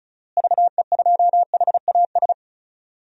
Morse code, heard as a keyed tone, about 36 wpm, 700 Hz. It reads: VE2HAS
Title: DXCC-100 Call Signs - 6 Characters - Repeated 3x - 35wpm